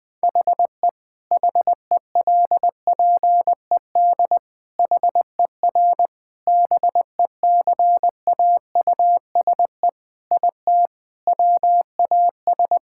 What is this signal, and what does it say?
Morse code, heard as a keyed tone, about 20 wpm, 700 Hz. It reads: HE HELPED HER BECAUSE IT WAS
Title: Sentences with 8-Word Phrases - 20wpm